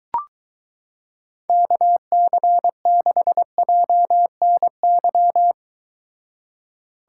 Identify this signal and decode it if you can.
Morse code, heard as a keyed tone, about 23 wpm, 700 Hz. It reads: KC6JNY